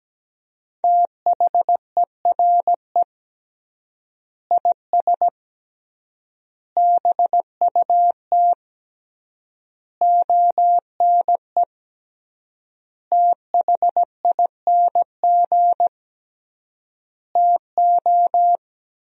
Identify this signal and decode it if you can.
Morse code, heard as a keyed tone, about 17 wpm, 700 Hz. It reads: THERE IS BUT ONE THING TO
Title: 3x Word-Spacing - Sentences from Top 100 Words - 17wpm